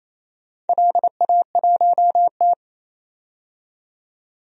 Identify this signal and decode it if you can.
Morse code, heard as a keyed tone, about 28 wpm, 700 Hz. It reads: LA1T
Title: DXCC-100 Call Signs - 4 Characters - 28wpm